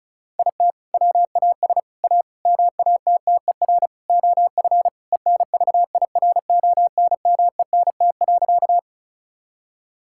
Morse code, heard as a keyed tone, about 36 wpm, 700 Hz. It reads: IT WAS A MATTER OF ENVIRONMENT.
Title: Sentences from Top 1000 Words - Review Part 5 - 35wpm